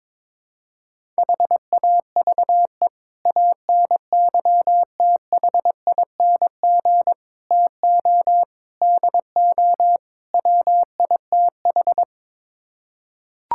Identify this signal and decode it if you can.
Morse code, heard as a keyed tone, about 22 wpm, 700 Hz. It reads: HAVE ANYTHING TO DO WITH
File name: practice / 22wpm / Build A Phrase - 5-Word Phrases - 22wpm